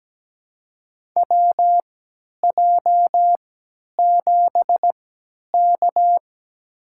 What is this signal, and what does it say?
Morse code, heard as a keyed tone, about 17 wpm, spaced as for 10 wpm, 700 Hz. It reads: WJ7K
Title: USA Call Signs - 3x Character Spacing - 17wpm